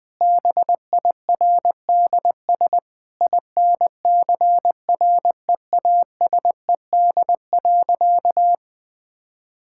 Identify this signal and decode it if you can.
Morse code, heard as a keyed tone, about 20 wpm, 700 Hz. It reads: BIRDS INCREASED.
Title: Sentences with 3-Word Phrases - Spoken Beginning - 20wpm